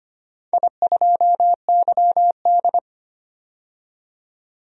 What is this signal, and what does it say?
Morse code, heard as a keyed tone, about 25 wpm, 700 Hz. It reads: I2YD
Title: DXCC-100 Call Signs - 4 Characters - 25wpm